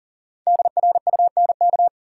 Morse code, heard as a keyed tone, about 40 wpm, 700 Hz. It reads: DRUNK
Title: Five Letter Words - 40wpm